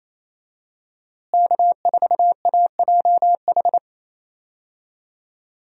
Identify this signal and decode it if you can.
Morse code, heard as a keyed tone, about 28 wpm, 700 Hz. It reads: K4AJH
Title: USA Call Signs - 5 Characters - Repeated 2x - 28wpm